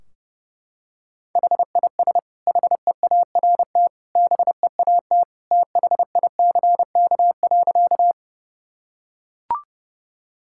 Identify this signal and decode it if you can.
Morse code, heard as a keyed tone, about 30 wpm, 700 Hz. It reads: HIS HEART BEAT THICK.
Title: Sentences from Top 1600 Words 30wpm